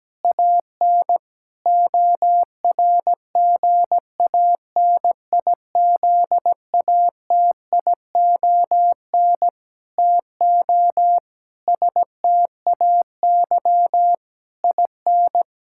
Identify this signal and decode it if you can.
Morse code, heard as a keyed tone, about 17 wpm, 700 Hz. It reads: AN ORGANIZATION TO STAY IN
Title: Sentences from Top 1000 Words - Review Part 5 - 17wpm